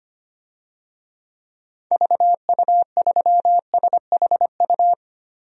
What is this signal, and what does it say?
Morse code, heard as a keyed tone, about 25 wpm, 700 Hz. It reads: VU3SHU
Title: DXCC-100 Call Signs - 6 Characters - Repeated 2x - 25wpm